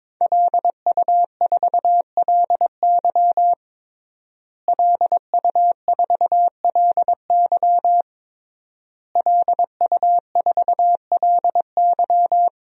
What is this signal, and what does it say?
Morse code, heard as a keyed tone, about 22 wpm, 700 Hz. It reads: LU4LY LU4LY LU4LY
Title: DXCC-100 Call Signs - 5 Characters - Repeated 3x - 22wpm